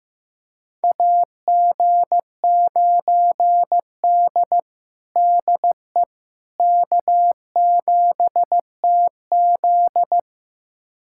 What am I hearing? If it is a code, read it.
Morse code, heard as a keyed tone, about 15 wpm, 700 Hz. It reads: AG9D DE K7TZ
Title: USA Callsign DE Callsign - 4 Characters - 15wpm